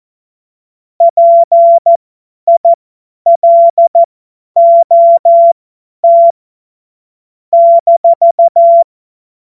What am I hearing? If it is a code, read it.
Morse code, heard as a keyed tone, about 14 wpm, spaced as for 10 wpm, 680 Hz. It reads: PILOT -
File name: Morsekurs / Witze / Witz 25. April 2023 50 F70